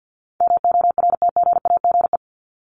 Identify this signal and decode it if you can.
Morse code, heard as a keyed tone, about 50 wpm, 700 Hz. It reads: MORTGAGE
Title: Top 1751-2000 Words 50wpm